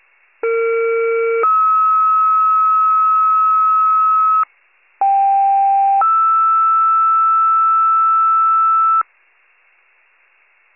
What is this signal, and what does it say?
Two-tone page, heard as a one-second tone, then a three-second tone, about 460 then 1290 Hz, about 790 then 1320 Hz. It is heard from 0.4 to 9.0 s.
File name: pager / 21 RESCUE / wcalert